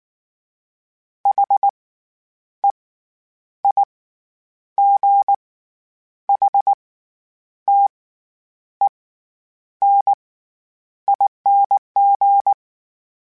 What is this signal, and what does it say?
Morse code, heard as a keyed tone, about 19 wpm, spaced as for 9 wpm, 800 Hz. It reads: HEIGHTENING
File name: Special INGs / Special INGs-mp3s / SlowToFaster ing @18